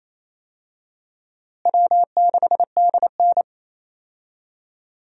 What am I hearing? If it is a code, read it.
Morse code, heard as a keyed tone, about 27 wpm, 700 Hz. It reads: W6DN
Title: USA Call Signs - 4 Characters - Repeated 2x - 28wpm